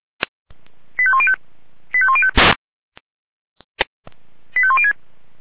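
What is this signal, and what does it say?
Five-tone selective call (ZVEI), heard as quick runs of five tones, tones of about 70 ms. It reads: 85105 85105 85106
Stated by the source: ZVEI five-tone selective call 85106 received by SDR.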